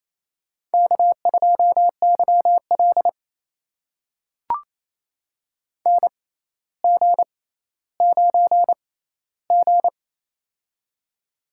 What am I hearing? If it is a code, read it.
Morse code, heard as a keyed tone, about 29 wpm, spaced as for 15 wpm, 700 Hz. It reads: K2YL NG9G
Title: USA Call Signs - 4 Characters - 6x Character Spacing - 28wpm